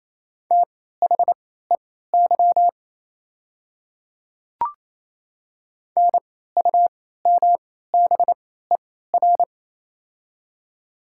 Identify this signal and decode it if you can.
Morse code, heard as a keyed tone, about 28 wpm, spaced as for 16 wpm, 700 Hz. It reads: THEY NUMBER
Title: 3x Character Spacing - Top 300 Words - 28wpm